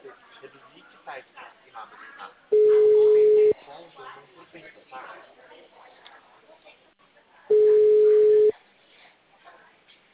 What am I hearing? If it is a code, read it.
Ringback tone (Europe, 425 Hz), with speech.